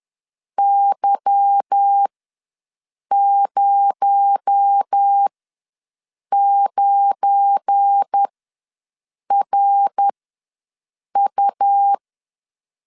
Morse code, heard as a keyed tone, about 10 wpm, spaced as for 6 wpm, 800 Hz. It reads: Y09RU
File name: LEZIONI / 8 YJ45 / 8 YJ ws 10wpm 4043